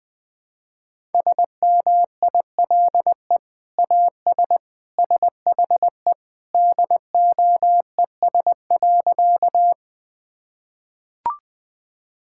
Morse code, heard as a keyed tone, about 21 wpm, 700 Hz. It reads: SMILE AS SHE DOES.
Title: Sentences with 2-Word Phrases - 20wpm